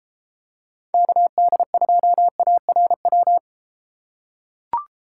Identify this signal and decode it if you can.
Morse code, heard as a keyed tone, about 31 wpm, 700 Hz. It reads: KD2ARW